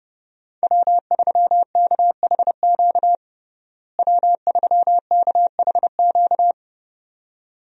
Morse code, heard as a keyed tone, about 31 wpm, 700 Hz. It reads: W3KHQ W3KHQ
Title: USA Call Signs - 5 Characters - Repeated 2x - 30wpm